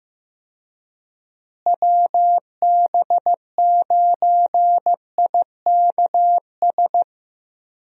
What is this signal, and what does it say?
Morse code, heard as a keyed tone, about 15 wpm, 700 Hz. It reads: WB9IKS
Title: USA Call Signs - 6 Characters - Repeated 2x - 15wpm